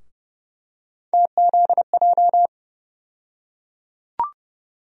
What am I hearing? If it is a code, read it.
Morse code, heard as a keyed tone, about 29 wpm, 700 Hz. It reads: TZJ
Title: USA Call Sign Postfixes - 3 Letters 30wpm